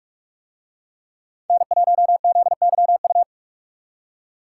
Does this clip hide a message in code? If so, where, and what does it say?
Yes: Morse code from 1.5 to 3.2 s, heard as a keyed tone, about 44 wpm, 700 Hz. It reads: N1ZYU